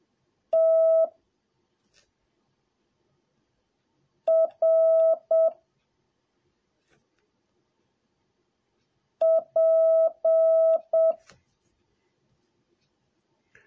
Morse code, heard as a keyed tone, about 7 wpm, 650 Hz. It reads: T R P